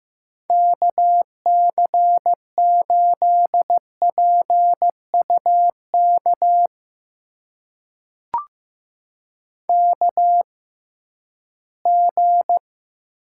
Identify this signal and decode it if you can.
Morse code, heard as a keyed tone, about 15 wpm, spaced as for 11 wpm, 700 Hz. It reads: KC8PUK KG
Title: USA Call Signs - 6 Characters - 6x Character Spacing - 15wpm